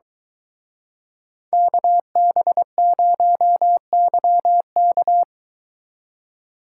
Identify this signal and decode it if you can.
Morse code, heard as a keyed tone, about 22 wpm, 700 Hz. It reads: KB0YK